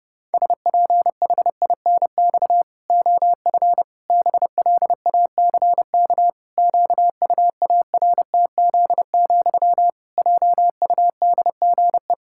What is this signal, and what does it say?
Morse code, heard as a keyed tone, about 30 wpm, 700 Hz. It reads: SPHINX OF BLACK QUARTZ, JUDGE
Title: Pangram Challenge v2 - 30wpm